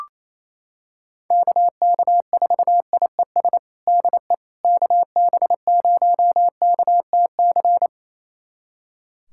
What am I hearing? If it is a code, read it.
Morse code, heard as a keyed tone, about 28 wpm, 700 Hz. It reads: KK4IES DE KB0KTC
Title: USA Callsign DE Callsign 28wpm